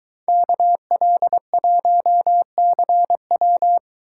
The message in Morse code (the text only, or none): KL1CW